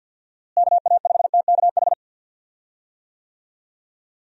Morse code, heard as a keyed tone, about 46 wpm, 700 Hz. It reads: KA5TKH
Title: USA Call Signs - 6 Characters - Repeated 2x - 50wpm